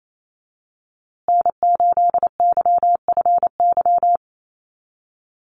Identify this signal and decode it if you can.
Morse code, heard as a keyed tone, about 28 wpm, 700 Hz. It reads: N8YFY
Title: USA Call Signs Repeated 2x 28wpm